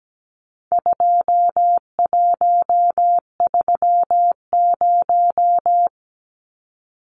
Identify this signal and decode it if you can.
Morse code, heard as a keyed tone, about 17 wpm, 700 Hz. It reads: 2130
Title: Numbers - 4-Digit 17wpm